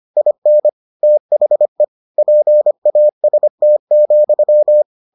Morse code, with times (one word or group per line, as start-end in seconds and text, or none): 0.2-0.7 IN
1.0-1.8 THE
2.2-4.8 PAST,